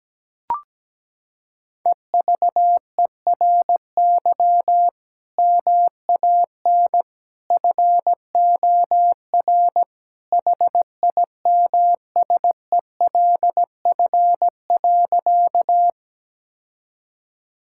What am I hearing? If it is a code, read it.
Morse code, heard as a keyed tone, about 17 wpm, 700 Hz. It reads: EVERY MAN FOR HIMSELF.
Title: Sentences from Top 1600 Words 17wpm